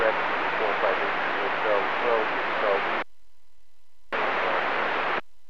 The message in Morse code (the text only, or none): none